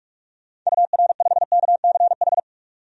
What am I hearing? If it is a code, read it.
Morse code, heard as a keyed tone, about 46 wpm, 700 Hz. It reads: UR5KCH
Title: DXCC-100 Call Signs - 6 Characters - 45wpm